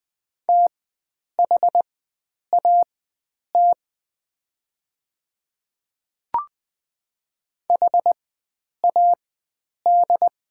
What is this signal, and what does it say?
Morse code, heard as a keyed tone, about 20 wpm, spaced as for 9 wpm, 700 Hz. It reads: THAT HAD
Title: 4x Character Spacing - Top 100 Words - 20wpm